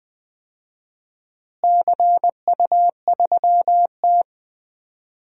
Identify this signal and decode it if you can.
Morse code, heard as a keyed tone, about 20 wpm, 700 Hz. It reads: CU3T
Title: DXCC-100 Call Signs - 4 Characters - Repeated 3x - 20wpm